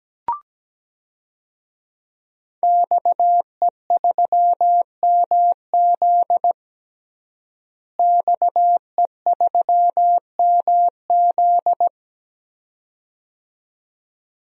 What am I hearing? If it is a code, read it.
Morse code, heard as a keyed tone, about 17 wpm, 700 Hz. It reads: XE3MZ XE3MZ
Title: DXCC-100 Call Signs - 5 Characters - Repeated 2x - 17wpm